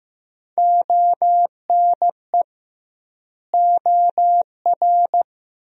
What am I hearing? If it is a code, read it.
Morse code, heard as a keyed tone, about 15 wpm, 700 Hz. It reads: ONE OR
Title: 2x Word-Spacing - Sentences from Top 100 Words - 15wpm